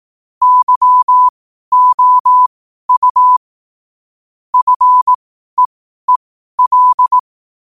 Morse code, heard as a keyed tone, about 18 wpm, spaced as for 12 wpm, 1000 Hz. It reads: YOU FEEL